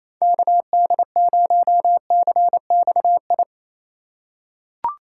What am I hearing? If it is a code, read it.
Morse code, heard as a keyed tone, about 28 wpm, 700 Hz. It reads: KD0CXI
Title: USA Call Signs - 2x Character Spacing - 28wpm